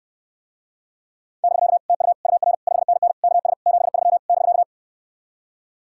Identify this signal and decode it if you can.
Morse code, heard as a keyed tone, about 69 wpm, 700 Hz. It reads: <BK> TU UR 5NN KS 73 <BK>